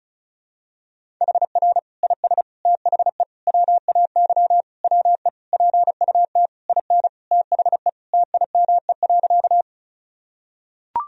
Morse code, heard as a keyed tone, about 34 wpm, 700 Hz. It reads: HR IS THE WAY WE PUT IN THE TIME.
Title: Sentences from Top 200 Words and Abbreviations 35wpm